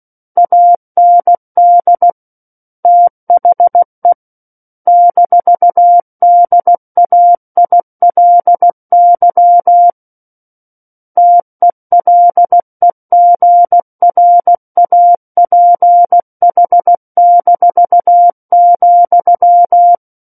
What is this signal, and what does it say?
Morse code, heard as a keyed tone, about 16 wpm, 700 Hz. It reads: AND THE -DAILY TELEGRAPH-,
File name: Book One / 07-Chapter 7 - 16 WPM